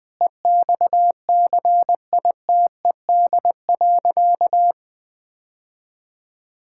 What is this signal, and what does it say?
Morse code, heard as a keyed tone, about 21 wpm, 700 Hz. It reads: EXCITED.
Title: Sentences with 5-Word Phrases - Spoken Beginning - 20wpm